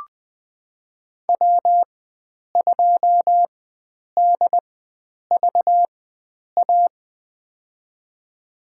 Morse code, heard as a keyed tone, about 20 wpm, spaced as for 9 wpm, 700 Hz. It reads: W2DVA
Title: USA Call Signs - 5 Characters - 4x Character Spacing - 20wpm